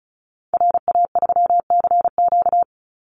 Morse code, heard as a keyed tone, about 34 wpm, 700 Hz. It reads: RA3CQ
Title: Call Signs - 35wpm